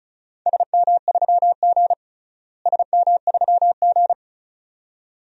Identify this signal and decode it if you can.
Morse code, heard as a keyed tone, about 36 wpm, 700 Hz. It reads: SM3G SM3G
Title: DXCC-100 Call Signs - 4 Characters - Repeated 2x - 35wpm